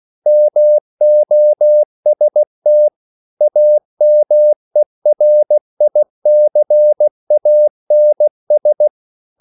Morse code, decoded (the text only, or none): MOST AMERICANS